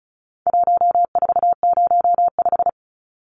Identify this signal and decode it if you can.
Morse code, heard as a keyed tone, about 34 wpm, 700 Hz. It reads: 1405
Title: Numbers - 4-Digit 35wpm